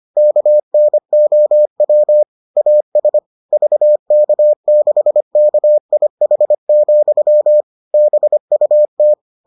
Morse code, heard as a keyed tone, about 25 wpm, 600 Hz. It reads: KNOW AS VK6KIH, BUT